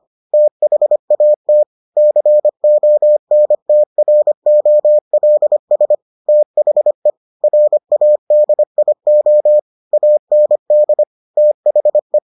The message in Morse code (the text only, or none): THAT CONTROLS THE RADIO AND THE